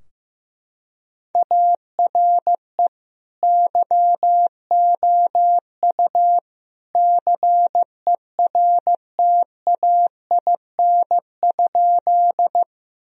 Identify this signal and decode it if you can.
Morse code, heard as a keyed tone, about 15 wpm, 700 Hz. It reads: ARE YOU CERTAIN?
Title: Sentences from Top 1000 Words - Review Part 2 - 15wpm